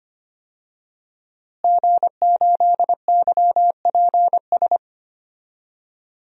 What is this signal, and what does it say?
Morse code, heard as a keyed tone, about 25 wpm, 700 Hz. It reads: G8YPS
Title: DXCC-100 Call Signs - 5 Characters - Repeated 2x - 25wpm